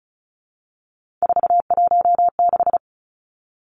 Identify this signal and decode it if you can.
Morse code, heard as a keyed tone, about 34 wpm, 700 Hz. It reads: 416
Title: Numbers - 3-Digit 35wpm